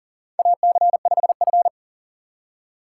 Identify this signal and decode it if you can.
Morse code, heard as a keyed tone, about 40 wpm, 700 Hz. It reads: AC5F
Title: USA Call Signs - 4 Characters - 6x Character Spacing - 40wpm